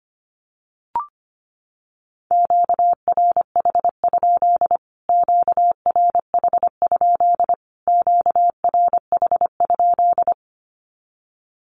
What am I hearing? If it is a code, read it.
Morse code, heard as a keyed tone, about 25 wpm, 700 Hz. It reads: QRH? QRH? QRH?
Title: Q-Codes Repeated 3x 25wpm